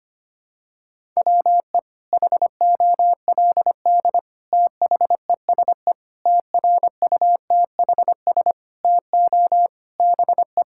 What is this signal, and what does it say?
Morse code, heard as a keyed tone, about 25 wpm, 700 Hz. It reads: WE HOLD THESE TRUTHS TO BE
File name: Sets of 9 Words - 25wpm